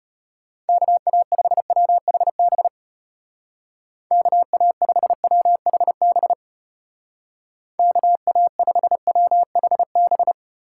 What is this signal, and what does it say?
Morse code, heard as a keyed tone, about 34 wpm, 700 Hz. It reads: KA5WHB KA5WHB KA5WHB